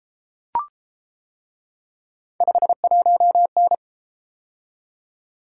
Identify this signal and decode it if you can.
Morse code, heard as a keyed tone, about 30 wpm, 700 Hz. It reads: 51N